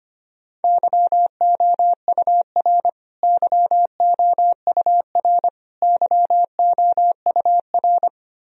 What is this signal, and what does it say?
Morse code, heard as a keyed tone, about 25 wpm, 700 Hz. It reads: YOUR YOUR YOUR
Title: Top 100 Words Repeated 25wpm